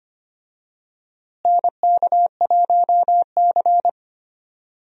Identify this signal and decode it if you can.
Morse code, heard as a keyed tone, about 25 wpm, 700 Hz. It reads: NK1C
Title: USA Call Signs - 4 Characters - Repeated 2x - 25wpm